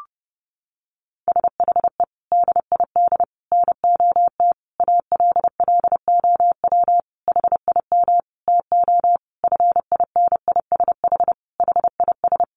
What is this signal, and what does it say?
Morse code, heard as a keyed tone, about 30 wpm, 700 Hz. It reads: SHE DID NOT ALLOW HIM TO FINISH HIS
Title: Sentences from Top 1200 Words 30wpm